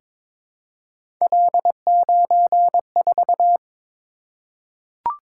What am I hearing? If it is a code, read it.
Morse code, heard as a keyed tone, about 22 wpm, 700 Hz. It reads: L94
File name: practice / 22wpm / Three Letter-Number Combinations - 22wpm